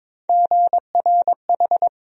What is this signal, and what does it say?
Morse code, heard as a keyed tone, about 22 wpm, 700 Hz. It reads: GRH
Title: USA Call Sign Postfixes - 3 Letters 22wpm